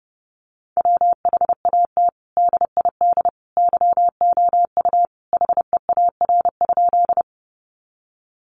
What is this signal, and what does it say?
Morse code, heard as a keyed tone, about 28 wpm, 700 Hz. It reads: WHAT DID YOU HEAR?
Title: Sentences from Top 300 Words 30wpm